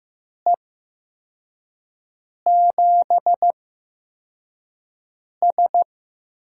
Morse code, heard as a keyed tone, about 15 wpm, spaced as for 4 wpm, 700 Hz. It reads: E7S
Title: USA Call Signs - 6 Characters - 8x Character Spacing - 15wpm